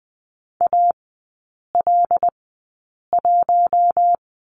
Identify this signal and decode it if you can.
Morse code, heard as a keyed tone, about 20 wpm, spaced as for 8 wpm, 700 Hz. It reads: AL1
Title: 074 Lesson 13e - Letter L - Three Characters 20wpm